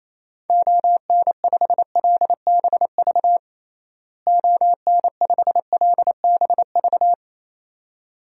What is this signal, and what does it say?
Morse code, heard as a keyed tone, about 28 wpm, 700 Hz. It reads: ON5LBV ON5LBV
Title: DXCC-100 Call Signs - 6 Characters - Repeated 2x - 28wpm